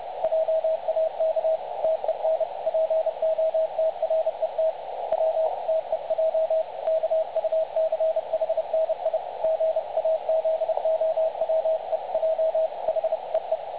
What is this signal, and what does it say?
Morse code, heard as a keyed tone, about 29 wpm, 660 Hz. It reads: JAK TEN POTRET W TEJ KUCHNI GAZOWEJ HEE
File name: dlazaawansowanych / nagraniazpasma / websdr recording 3565.0kHz